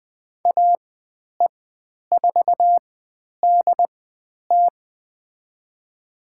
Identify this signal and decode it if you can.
Morse code, heard as a keyed tone, about 20 wpm, spaced as for 10 wpm, 700 Hz. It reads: AE4DT
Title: 095 Lesson 16g - Letter c - Call Signs 20wpm